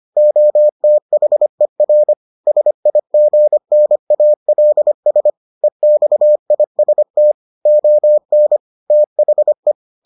Morse code, heard as a keyed tone, about 25 wpm, 600 Hz. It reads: OTHER SIGNALS EXIST ON THE